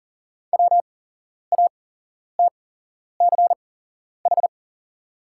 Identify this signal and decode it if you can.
Morse code, heard as a keyed tone, about 38 wpm, spaced as for 11 wpm, 700 Hz. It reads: WATCH
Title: 8x Character Spacing - Top 300 Words - 40wpm